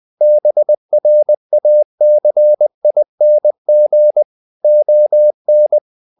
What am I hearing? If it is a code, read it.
Morse code, heard as a keyed tone, about 20 wpm, 600 Hz. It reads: BRACING ON
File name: the old man and the sea / the old man and the sea part 1 20wpm